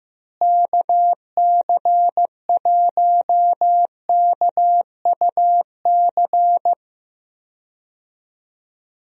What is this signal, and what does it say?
Morse code, heard as a keyed tone, about 15 wpm, 700 Hz. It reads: KC1KUC
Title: USA Call Signs - 6 Characters - Repeated 3x - 15wpm